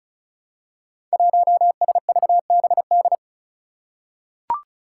Morse code, heard as a keyed tone, about 36 wpm, 700 Hz. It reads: 1SVBD
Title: Five Challenging Letter-Number Combinations 35wpm